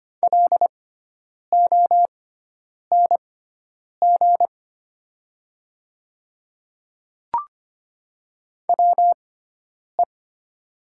Morse code, heard as a keyed tone, about 25 wpm, spaced as for 9 wpm, 700 Hz. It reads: LONG WE
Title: 6x Character Spacing - Top 100 Words - 25wpm